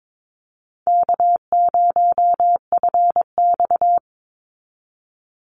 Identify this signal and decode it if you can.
Morse code, heard as a keyed tone, about 22 wpm, 700 Hz. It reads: K0FX
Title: Easy Call Signs Repeated 2x - 22wpm